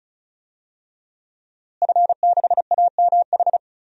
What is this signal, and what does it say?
Morse code, heard as a keyed tone, about 34 wpm, 700 Hz. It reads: F6AMH
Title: DXCC-100 Call Signs - 5 Characters - Repeated 2x - 35wpm